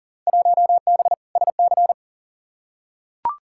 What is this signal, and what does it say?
Morse code, heard as a keyed tone, about 40 wpm, 700 Hz. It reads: 1B SC
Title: Field Day Exchanges - 40wpm